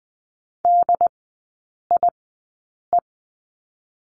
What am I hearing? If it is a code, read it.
Morse code, heard as a keyed tone, about 20 wpm, spaced as for 8 wpm, 700 Hz. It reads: DIE
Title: 074 Lesson 13e - Letter L - Three Characters 20wpm